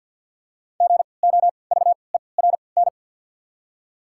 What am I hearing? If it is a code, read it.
Morse code, heard as a keyed tone, about 53 wpm, spaced as for 29 wpm, 690 Hz. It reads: GOVERN
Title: 3x Character Spacing - Top 500 Words - 50wpm